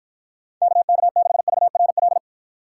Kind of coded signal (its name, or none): Morse code